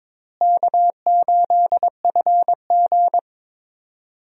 Morse code, heard as a keyed tone, about 22 wpm, 700 Hz. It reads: K8FG
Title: USA Call Signs - 4 Characters - Repeated 3x - 22wpm